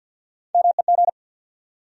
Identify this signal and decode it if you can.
Morse code, heard as a keyed tone, about 54 wpm, 700 Hz. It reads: MEG
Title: USA Common Names - Max 3 Letters - 50wpm